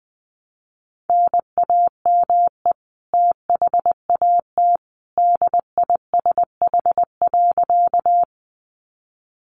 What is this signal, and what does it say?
Morse code, heard as a keyed tone, about 20 wpm, 700 Hz. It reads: NAME THAT DISH.
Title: Sentences from Top 1300 Words 20wpm